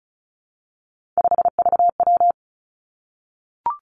Morse code, heard as a keyed tone, about 36 wpm, 700 Hz. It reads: 5VW
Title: Three Challenging Letter-Number Combinations 35wpm